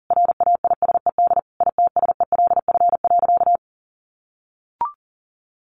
Morse code, heard as a keyed tone, about 40 wpm, 700 Hz. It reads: RAISED ITSELF.
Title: A Sound Of Thunder 40wpm